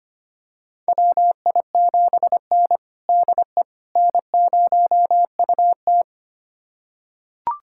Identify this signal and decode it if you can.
Morse code, heard as a keyed tone, about 25 wpm, 700 Hz. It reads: WI7N DE N0UT
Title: USA Callsign DE Callsign - 4 Characters - 25wpm